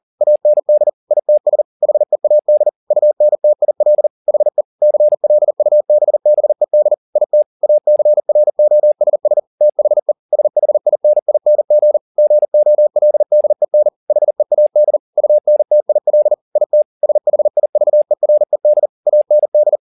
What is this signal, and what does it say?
Morse code, heard as a keyed tone, about 40 wpm, 600 Hz. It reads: AND ITS HEAD UNTIL HE CLUBBED IT ACROSS THE SHINING GOLDEN HEAD UNTIL IT SHIVERED AND
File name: the old man and the sea 2 40wpm